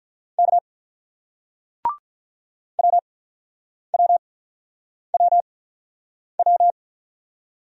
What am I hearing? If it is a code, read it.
Morse code, heard as a keyed tone, about 48 wpm, spaced as for 9 wpm, 700 Hz. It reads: K WWWW